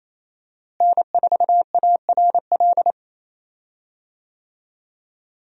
Morse code, heard as a keyed tone, about 28 wpm, 700 Hz. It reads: N4ARL
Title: USA Call Signs - 5 Characters - Repeated 2x - 28wpm